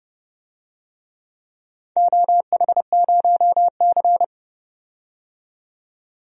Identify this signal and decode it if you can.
Morse code, heard as a keyed tone, about 30 wpm, 700 Hz. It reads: OH0C
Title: DXCC-100 Call Signs - 4 Characters - 30wpm